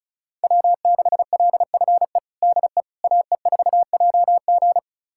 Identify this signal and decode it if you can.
Morse code, heard as a keyed tone, about 36 wpm, 700 Hz. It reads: W6LFE DE AE4JG
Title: USA Callsign DE Callsign - 5 Characters - 35wpm